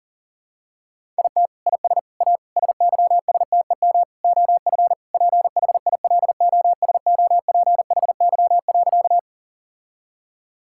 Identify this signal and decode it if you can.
Morse code, heard as a keyed tone, about 37 wpm, 690 Hz. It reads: IT IS A SYSTEM OF PHILOSOPHY.
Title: Sentences from Top 1000 Words - Review Part 6 - 40wpm